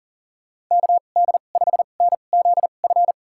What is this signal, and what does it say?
Morse code, heard as a keyed tone, about 40 wpm, spaced as for 29 wpm, 700 Hz. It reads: KD5NZF
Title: USA Call Signs - 2x Character Spacing - 40wpm